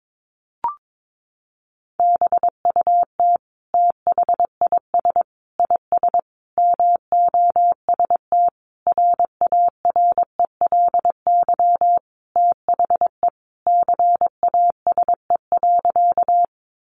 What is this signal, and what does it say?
Morse code, heard as a keyed tone, about 22 wpm, 700 Hz. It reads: BUT THIS IS MOST RARELY THE CASE.
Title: Sentences from Top 1400 Words 22wpm